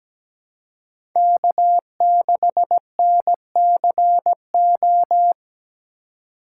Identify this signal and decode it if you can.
Morse code, heard as a keyed tone, about 17 wpm, 700 Hz. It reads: K6NCO
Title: USA Call Signs - 5 Characters - 2x Character Spacing - 17wpm